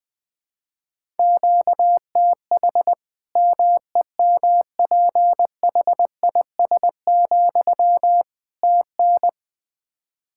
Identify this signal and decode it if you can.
Morse code, heard as a keyed tone, about 20 wpm, 700 Hz. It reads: QTH MEMPHIS, TN
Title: Practice QSO - Exchange 1 - Beginning - 20wpm